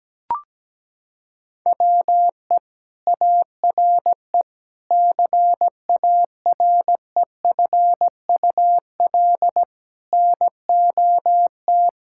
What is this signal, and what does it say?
Morse code, heard as a keyed tone, about 17 wpm, 700 Hz. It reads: WE ARE CAREFUL NOT
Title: Sentences from Top 1600 Words 17wpm